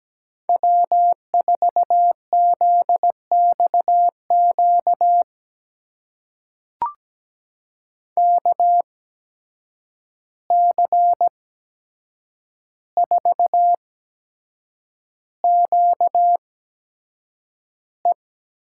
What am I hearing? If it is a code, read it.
Morse code, heard as a keyed tone, about 17 wpm, spaced as for 7 wpm, 700 Hz. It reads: W4ZXQ KC4QE
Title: USA Call Signs - 5 Characters - 8x Character Spacing - 17wpm